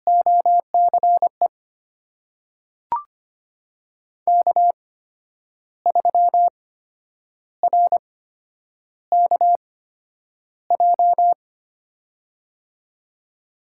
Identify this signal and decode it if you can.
Morse code, heard as a keyed tone, about 25 wpm, spaced as for 9 wpm, 700 Hz. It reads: OCE K3RKJ
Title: USA Call Signs - 5 Characters - 8x Character Spacing - 25wpm